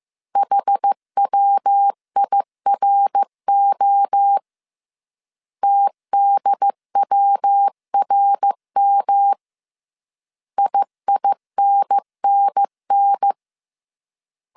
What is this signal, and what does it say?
Morse code, heard as a keyed tone, about 15 wpm, 800 Hz. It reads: HWIRO TDWRM IINNN